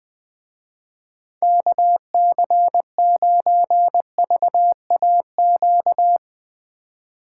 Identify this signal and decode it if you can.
Morse code, heard as a keyed tone, about 20 wpm, 700 Hz. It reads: KC9VAQ
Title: Speed Racing - USA 2x3 Call Signs 17wpm